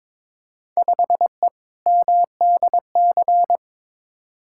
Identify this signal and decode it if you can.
Morse code, heard as a keyed tone, about 22 wpm, 700 Hz. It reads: 5E MDC
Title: Field Day Exchanges - 22wpm